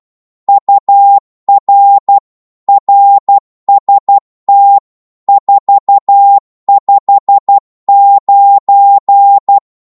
Morse code, heard as a keyed tone, about 12 wpm, 800 Hz. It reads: UR RST 459
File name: Morse / qso-12wpm